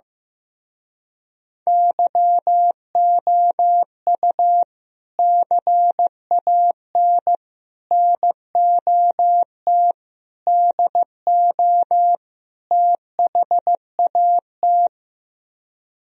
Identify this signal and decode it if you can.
Morse code, heard as a keyed tone, about 15 wpm, 700 Hz. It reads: YOU CAN NOT DO THAT
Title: Speed-Racing - Sets of 5 Words 15wpm